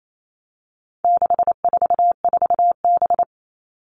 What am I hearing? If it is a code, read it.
Morse code, heard as a keyed tone, about 28 wpm, 700 Hz. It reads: 644B
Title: All Cut Numbers - 4-Digit 28wpm